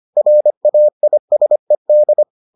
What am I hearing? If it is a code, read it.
Morse code, heard as a keyed tone, about 25 wpm, 600 Hz. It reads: RAISED